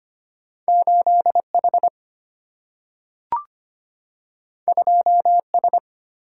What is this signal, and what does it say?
Morse code, heard as a keyed tone, about 25 wpm, 700 Hz. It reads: 8 H 2 S
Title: Single-Letter Single-Number 25wpm